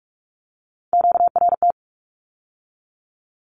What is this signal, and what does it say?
Morse code, heard as a keyed tone, about 46 wpm, 700 Hz. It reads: QRT
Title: Q-Codes with Spoken Meaning - 45wpm